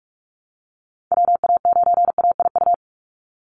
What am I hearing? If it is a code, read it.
Morse code, heard as a keyed tone, about 46 wpm, 700 Hz. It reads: WA9AIU
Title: USA Call Signs 45wpm